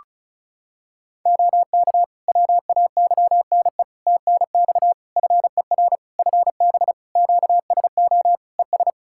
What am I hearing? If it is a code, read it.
Morse code, heard as a keyed tone, about 34 wpm, 700 Hz. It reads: OK WAYNE TNX FER FB QSO ES
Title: Practice QSO - Exchange 4 - Ending - 35wpm